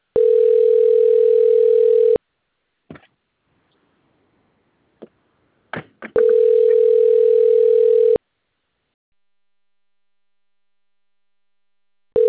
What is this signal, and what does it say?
Ringback tone (North America), 440 + 480 Hz.
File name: rg-1600-unknown- - - .4120